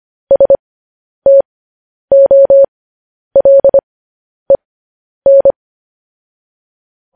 Morse code, heard as a keyed tone, about 25 wpm, spaced as for 10 wpm, 550 Hz. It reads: STOLEN
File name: podcast / News headlines-1x-25x10